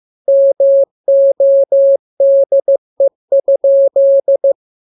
Morse code, heard as a keyed tone, about 15 wpm, 550 Hz. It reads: MODE?
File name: jokes / joke 27 15 0000